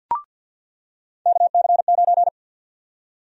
Morse code, heard as a keyed tone, about 50 wpm, 700 Hz. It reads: KC9
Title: USA Call Sign Prefixes - 50wpm